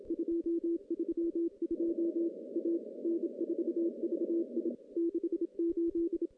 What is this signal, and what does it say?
Morse code, heard as a keyed tone, about 28 wpm, 340 Hz. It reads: 232 A N4VI 68
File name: contests / CW-SS-k5tr- -0700